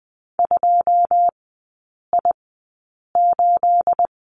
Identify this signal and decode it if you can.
Morse code, heard as a keyed tone, about 20 wpm, spaced as for 8 wpm, 690 Hz. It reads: 2I8
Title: 212 Lesson 34e - Number 8 - Three Characters 20wpm